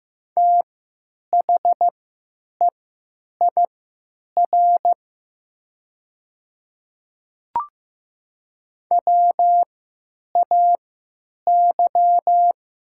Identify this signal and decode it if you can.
Morse code, heard as a keyed tone, about 15 wpm, spaced as for 9 wpm, 700 Hz. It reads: THEIR WAY
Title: 3x Character Spacing - Top 100 Words - 15wpm